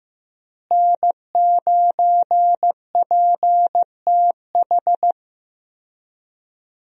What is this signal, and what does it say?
Morse code, heard as a keyed tone, about 15 wpm, 700 Hz. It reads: N9PTH
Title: USA Call Signs - 5 Characters - Repeated 2x - 15wpm